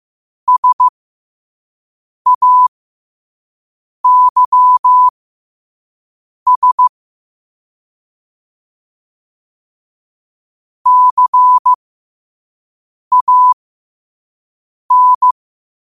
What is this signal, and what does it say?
Morse code, heard as a keyed tone, about 15 wpm, spaced as for 5 wpm, 1000 Hz. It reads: SAYS CAN
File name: qotdfiles / 5wpm